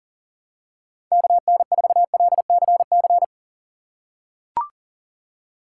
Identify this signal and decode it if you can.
Morse code, heard as a keyed tone, about 39 wpm, 700 Hz. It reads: KN4LCC
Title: USA Call Signs - 6 Characters - 4x Character Spacing - 40wpm